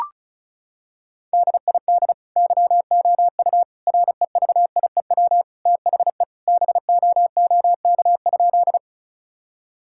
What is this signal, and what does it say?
Morse code, heard as a keyed tone, about 36 wpm, 700 Hz. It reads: DID YOU REVIEW THE BOOK?
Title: Sentences from Top 1000 Words - Review Part 6 - 35wpm